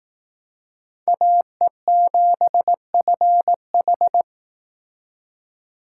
Morse code, heard as a keyed tone, about 18 wpm, 700 Hz. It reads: AE7FH
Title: Speed Racing - USA 2x2 Call Signs 15wpm